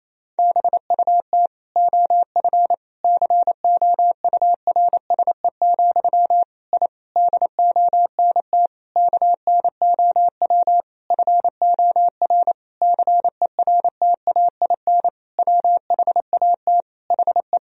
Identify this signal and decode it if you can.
Morse code, heard as a keyed tone, about 27 wpm, 700 Hz. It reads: BUT OF COURSE, I DONT KNOW FOR CERTAIN WHAT HE
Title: Sentences from Top 1000 Words - Review Part 2 - 28wpm